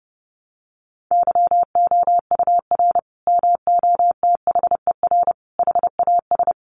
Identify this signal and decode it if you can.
Morse code, heard as a keyed tone, about 28 wpm, 700 Hz. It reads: YOUR MOTHER HAS
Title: Sentences from Top 1200 Words 30wpm